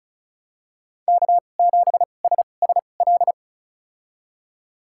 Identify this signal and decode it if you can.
Morse code, heard as a keyed tone, about 36 wpm, spaced as for 25 wpm, 700 Hz. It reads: K7SSL
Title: USA Call Signs - 2x Character Spacing - 35wpm